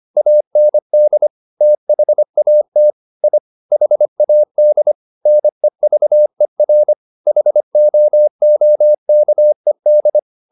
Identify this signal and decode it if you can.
Morse code, heard as a keyed tone, about 25 wpm, 600 Hz. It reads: AND THAT I HAD NEVER HOOKED